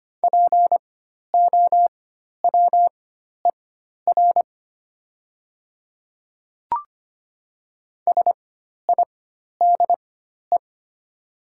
Morse code, heard as a keyed tone, about 25 wpm, spaced as for 12 wpm, 700 Hz. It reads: POWER SIDE
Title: 4x Character Spacing - Top 500 Words - 25wpm